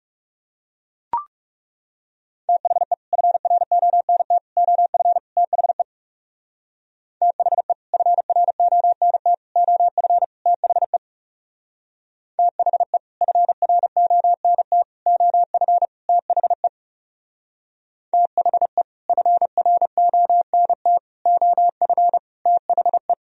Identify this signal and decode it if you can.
Morse code, heard as a keyed tone, about 38 wpm, 700 Hz. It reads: THE FRONT OF THE THE FRONT OF THE THE FRONT OF THE THE FRONT OF THE